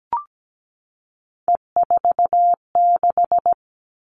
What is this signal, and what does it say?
Morse code, heard as a keyed tone, about 17 wpm, 700 Hz. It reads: E46
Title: All Cut Numbers - 3-Digit 17wpm